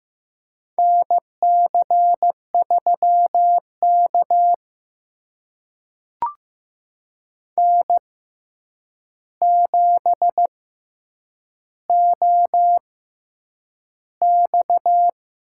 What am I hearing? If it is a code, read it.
Morse code, heard as a keyed tone, about 15 wpm, spaced as for 8 wpm, 700 Hz. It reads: NC3K N7OX
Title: USA Call Signs - 4 Characters - 6x Character Spacing - 15wpm